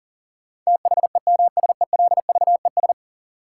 Morse code, heard as a keyed tone, about 40 wpm, 700 Hz. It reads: THEMSELVES
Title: Sentences from Top 800 Words - New Word Drill - 40wpm